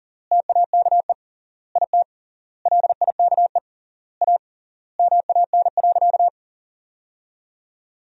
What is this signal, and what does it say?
Morse code, heard as a keyed tone, about 40 wpm, 700 Hz. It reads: TAKE IT LIKE A MAN.
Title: 3x Word-Spacing - Sentences from Top 300 Words - 40wpm